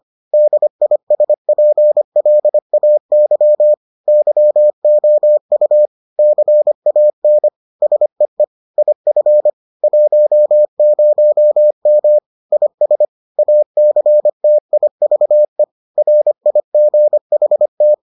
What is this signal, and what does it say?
Morse code, heard as a keyed tone, about 25 wpm, 600 Hz. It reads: DISPLAY YOU CAN SEE IF 10M IS ACTIVE RIGHT